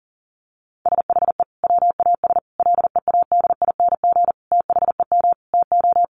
Morse code, heard as a keyed tone, about 40 wpm, 700 Hz. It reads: SHE WAS LEADING THEM TO
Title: Sentences from Top 1100 Words 40wpm